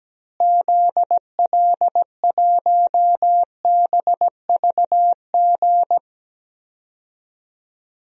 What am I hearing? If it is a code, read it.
Morse code, heard as a keyed tone, about 17 wpm, 700 Hz. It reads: ZL1BVG
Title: DXCC-100 Call Signs - 6 Characters - Repeated 3x - 17wpm